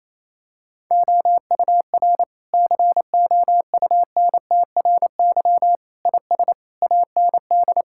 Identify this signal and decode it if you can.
Morse code, heard as a keyed tone, about 28 wpm, 700 Hz. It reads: OUR COUNTRY IS AND
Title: Sentences from Top 1000 Words - Review Part 6 - 28wpm